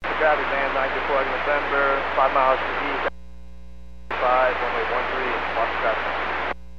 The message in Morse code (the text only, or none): none